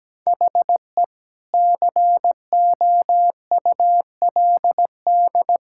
Morse code, heard as a keyed tone, about 17 wpm, 700 Hz. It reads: HE COULD
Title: Sentences from Top 1600 Words 17wpm